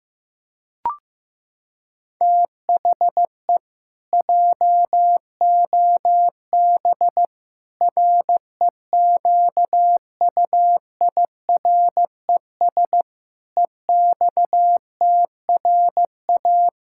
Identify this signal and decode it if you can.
Morse code, heard as a keyed tone, about 16 wpm, 700 Hz. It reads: THE JOB REQUIRES EXTRA
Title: Pangram Challenge v2 - 15wpm